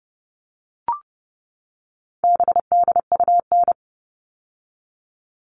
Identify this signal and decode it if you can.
Morse code, heard as a keyed tone, about 30 wpm, 700 Hz. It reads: BDUN
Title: All Cut Numbers - 4-Digit 30wpm